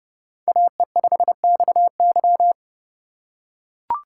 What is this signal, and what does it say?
Morse code, heard as a keyed tone, about 30 wpm, 700 Hz. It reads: AE5XY
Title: USA Call Signs - 5 Characters - 3x Character Spacing - 30wpm